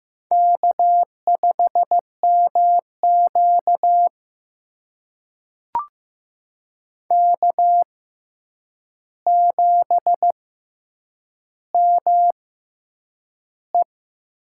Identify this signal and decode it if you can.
Morse code, heard as a keyed tone, about 15 wpm, spaced as for 8 wpm, 700 Hz. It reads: K5MQ K7ME
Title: USA Call Signs - 4 Characters - 6x Character Spacing - 15wpm